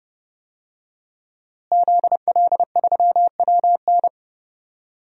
Morse code, heard as a keyed tone, about 30 wpm, 700 Hz. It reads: ZL3WN
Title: DXCC-100 Call Signs - 5 Characters - Repeated 3x - 30wpm